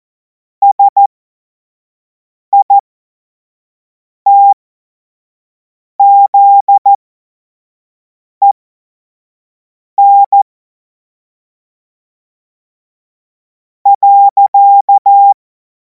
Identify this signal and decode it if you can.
Morse code, heard as a keyed tone, about 14 wpm, spaced as for 4 wpm, 800 Hz. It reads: SITZEN.
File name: Morsekurs / Witze / Witz 17. Februar 2026 25 F70